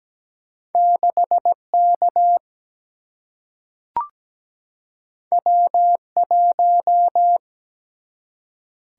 Morse code, heard as a keyed tone, about 17 wpm, 700 Hz. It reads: 6K W1
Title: Single-Letter Single-Number 17wpm